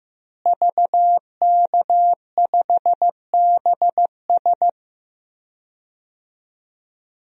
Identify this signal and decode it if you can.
Morse code, heard as a keyed tone, about 15 wpm, 700 Hz. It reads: VK5BS
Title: DXCC-100 Call Signs - 5 Characters - Repeated 2x - 15wpm